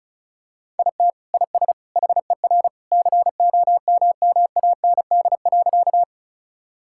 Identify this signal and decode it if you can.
Morse code, heard as a keyed tone, about 34 wpm, 700 Hz. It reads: IT IS HER COMMAND.
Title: Sentences from Top 1600 Words 35wpm